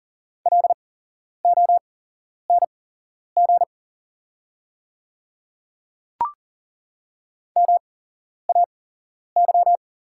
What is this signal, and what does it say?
Morse code, heard as a keyed tone, about 38 wpm, spaced as for 11 wpm, 700 Hz. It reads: LONG MAY
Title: 8x Character Spacing - Top 100 Words - 40wpm